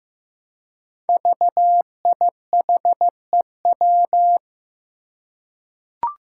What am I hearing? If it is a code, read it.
Morse code, heard as a keyed tone, about 15 wpm, 700 Hz. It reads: VIHEW
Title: Code Groups - 5 Letter:Numbers - 15wpm